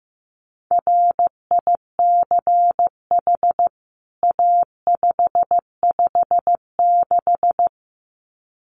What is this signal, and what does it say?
Morse code, heard as a keyed tone, about 15 wpm, 700 Hz. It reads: RICH A556
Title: CWT Exchange 15wpm